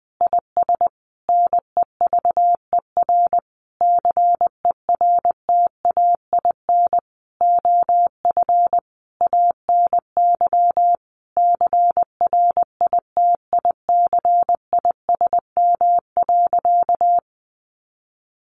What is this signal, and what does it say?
Morse code, heard as a keyed tone, about 20 wpm, 700 Hz. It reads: IS NEVER CERTAIN OF ANY CRITICISM.
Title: Sentences from Top 1300 Words 20wpm